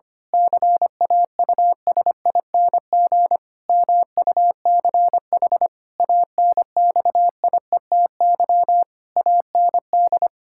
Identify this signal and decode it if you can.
Morse code, heard as a keyed tone, about 25 wpm, 700 Hz. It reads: CAUSING MUCH ANXIETY AND
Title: Pangram Challenge v2 - 25wpm